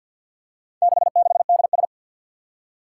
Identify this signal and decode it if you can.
Morse code, heard as a keyed tone, about 52 wpm, 700 Hz. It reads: 66DS
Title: Four Challenging Letter-Number Combinations 50wpm